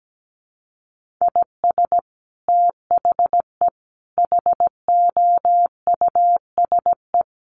Morse code, heard as a keyed tone, about 17 wpm, 700 Hz. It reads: IS THE HOUSE
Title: Sentences from Top 1400 Words 17wpm